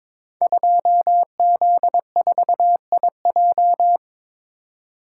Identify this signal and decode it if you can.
Morse code, heard as a keyed tone, about 22 wpm, 700 Hz. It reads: 2Z4IJ
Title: Code Groups - 5 Letter:Numbers - 22wpm